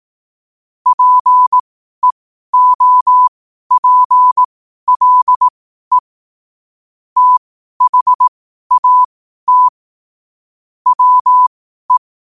Morse code, heard as a keyed tone, about 18 wpm, spaced as for 12 wpm, 1000 Hz. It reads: PEOPLE THAT WE
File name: qotdfiles / 12wpm